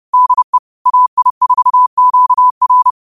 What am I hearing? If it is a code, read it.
Morse code, heard as a keyed tone, about 30 wpm, 1000 Hz. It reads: DE AI4QR
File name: qotdfiles / 30wpm